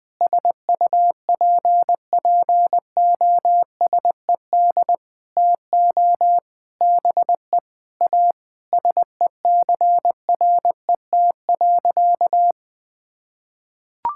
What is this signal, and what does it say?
Morse code, heard as a keyed tone, about 20 wpm, 700 Hz. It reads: SUPPOSED TO BE A SECRET.
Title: Sentences with 4-Word Phrases - 20wpm